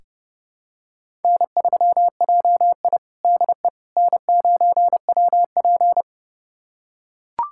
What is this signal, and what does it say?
Morse code, heard as a keyed tone, about 30 wpm, 700 Hz. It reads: N3JI DE N9WP
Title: USA Callsign DE Callsign - 4 Characters - 30wpm